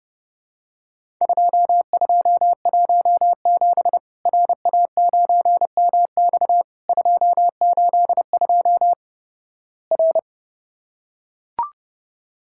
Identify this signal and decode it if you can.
Morse code, heard as a keyed tone, about 31 wpm, 700 Hz. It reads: 2217 RA9MX 282 R
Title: Worked All Europe DX Contest - QTC Exchange - 30wpm